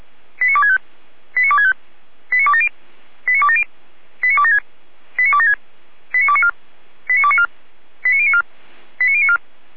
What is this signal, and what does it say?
Five-tone selective call (ZVEI), heard as quick runs of five tones, tones of about 70 ms. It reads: 89265 89265 89270 89270 89276 89276 89294 89294 89094 89094
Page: /455/SDR/ZVEI/89250/